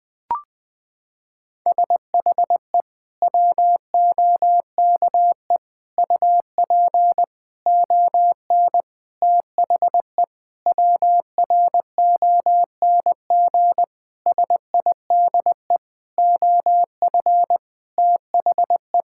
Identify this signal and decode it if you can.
Morse code, heard as a keyed tone, about 20 wpm, 700 Hz. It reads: SHE WOKE UP ON THE WRONG SIDE OF THE
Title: Sentences with 6-Word Phrases - 20wpm